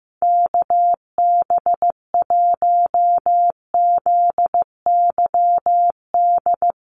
Morse code, heard as a keyed tone, about 15 wpm, 700 Hz. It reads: KB1ZYD
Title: USA Call Signs Repeated 3x 15wpm